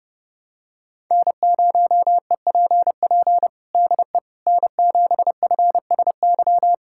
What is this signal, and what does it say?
Morse code, heard as a keyed tone, about 30 wpm, 700 Hz. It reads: N0EPP DE N7FSY
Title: USA Callsign DE Callsign - 5 Characters - 30wpm